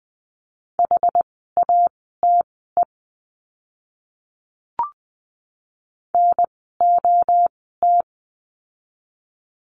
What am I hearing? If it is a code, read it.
Morse code, heard as a keyed tone, about 20 wpm, spaced as for 15 wpm, 700 Hz. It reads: HATE NOT
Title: 061 Lesson 11f - Letter h - Words 20wpm